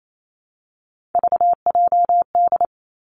Morse code, heard as a keyed tone, about 28 wpm, 690 Hz. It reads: VJD
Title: Three Challenging Letter-Number Combinations 28wpm